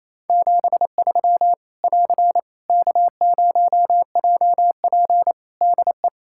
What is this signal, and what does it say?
Morse code, heard as a keyed tone, about 28 wpm, 700 Hz. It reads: 73 <AR> K0JP DE
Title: Practice QSO - Exchange 4 - Ending - 28wpm